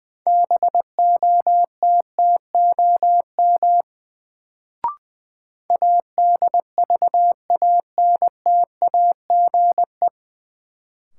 Morse code, heard as a keyed tone, about 21 wpm, 700 Hz. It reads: BOTTOM ADVANTAGE
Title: Sentences from Top 900 Words - New Word Drill - 20wpm